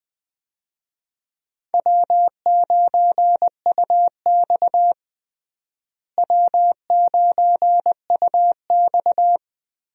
Morse code, heard as a keyed tone, about 20 wpm, 700 Hz. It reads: W9UX W9UX
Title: USA Call Signs - 4 Characters - Repeated 3x - 20wpm